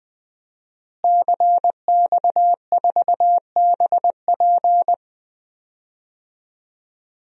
Morse code, heard as a keyed tone, about 20 wpm, 700 Hz. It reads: CX4BP
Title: DXCC-100 Call Signs - 5 Characters - Repeated 3x - 20wpm